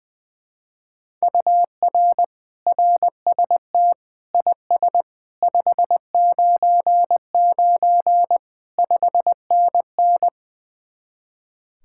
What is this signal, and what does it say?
Morse code, heard as a keyed tone, about 21 wpm, 700 Hz. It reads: UR RST IS 599 5NN
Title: Speed-Racing Common QSO Phrases Encore - 20wpm